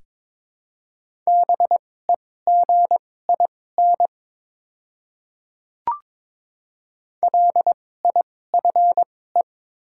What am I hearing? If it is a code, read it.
Morse code, heard as a keyed tone, about 22 wpm, spaced as for 16 wpm, 700 Hz. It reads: BEGIN LIFE
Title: 2x Character Spacing - Top 300 Words - 22wpm